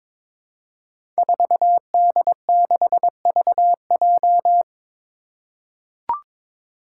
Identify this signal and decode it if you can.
Morse code, heard as a keyed tone, about 22 wpm, 700 Hz. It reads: 4D6VJ
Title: Five Challenging Letter-Number Combinations 22wpm